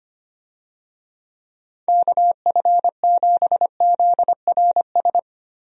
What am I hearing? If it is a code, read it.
Morse code, heard as a keyed tone, about 25 wpm, 700 Hz. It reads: KF7ZRS
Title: USA Call Signs - 6 Characters - 25wpm